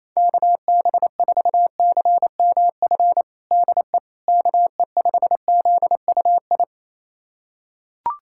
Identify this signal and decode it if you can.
Morse code, heard as a keyed tone, about 27 wpm, 700 Hz. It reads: KB4CMF DE KE5ZUI
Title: USA Callsign DE Callsign 28wpm